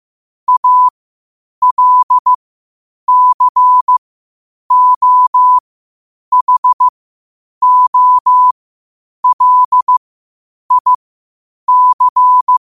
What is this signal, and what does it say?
Morse code, heard as a keyed tone, about 15 wpm, spaced as for 8 wpm, 1000 Hz. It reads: ALCOHOLIC
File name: qotdfiles / 8wpm